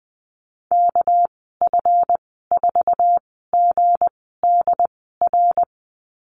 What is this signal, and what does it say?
Morse code, heard as a keyed tone, about 20 wpm, spaced as for 14 wpm, 700 Hz. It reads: KF4GDR
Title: 231 Lesson 37g - Letter x - Call Signs 20wpm